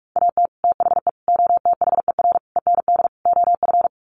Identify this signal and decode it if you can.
Morse code, heard as a keyed tone, about 44 wpm, 700 Hz. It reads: AT THE OTHER END OF
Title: Sets of 5 Words 45wpm